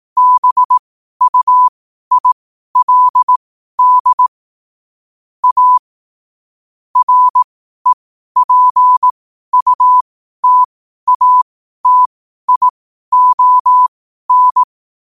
Morse code, heard as a keyed tone, about 18 wpm, spaced as for 12 wpm, 1000 Hz. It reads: BUILD A REPUTATION
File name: qotdfiles / 12wpm